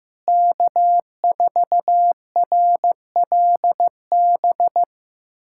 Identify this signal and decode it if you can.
Morse code, heard as a keyed tone, about 15 wpm, 700 Hz. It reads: K4RLB
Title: USA Callsign DE Callsign 15wpm